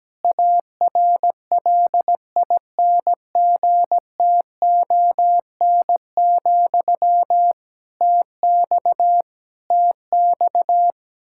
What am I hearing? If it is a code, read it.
Morse code, heard as a keyed tone, about 17 wpm, 700 Hz. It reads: ARLINGTON, TX TX
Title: Practice QSO - Exchange 1 - Beginning - 17wpm